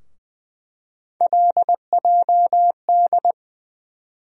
Morse code, heard as a keyed tone, about 20 wpm, 700 Hz. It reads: LJD
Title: USA Call Sign Postfixes - 3 Letters 20wpm